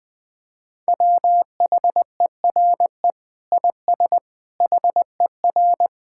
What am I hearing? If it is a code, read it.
Morse code, heard as a keyed tone, about 20 wpm, 700 Hz. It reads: WHERE IS HER
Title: Sentences from Top 1000 Words - Review Part 5 - 20wpm